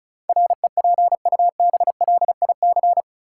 Morse code, heard as a keyed tone, about 34 wpm, 700 Hz. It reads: REPUBLIC
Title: Sentences from Top 1500 Words - New Word Drill Alt - 35wpm